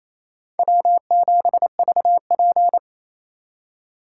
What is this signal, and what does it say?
Morse code, heard as a keyed tone, about 27 wpm, 700 Hz. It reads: W7VP
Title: K1USN SST QSO Practice - 28wpm